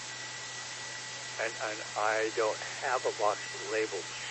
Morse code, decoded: none

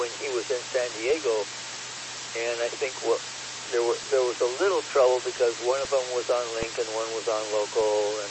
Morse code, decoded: none

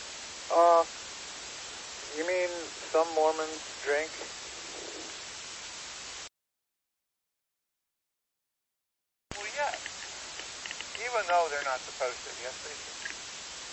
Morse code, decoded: none

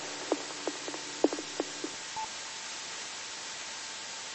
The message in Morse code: none

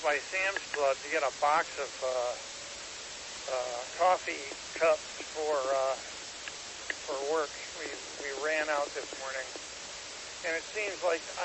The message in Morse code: none

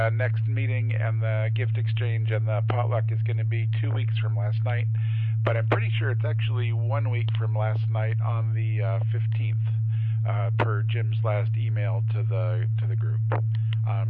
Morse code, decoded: none

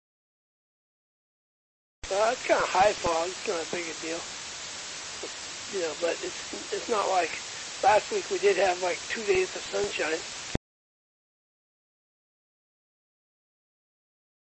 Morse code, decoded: none

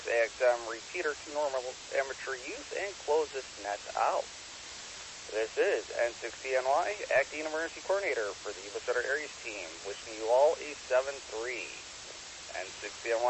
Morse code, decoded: none

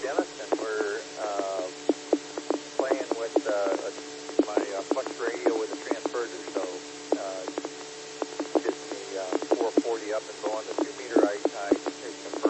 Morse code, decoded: none

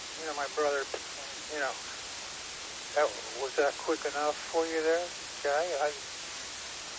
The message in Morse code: none

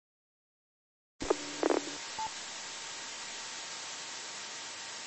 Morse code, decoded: none

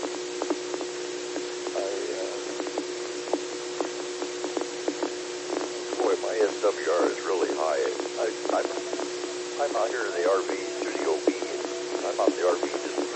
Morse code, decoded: none

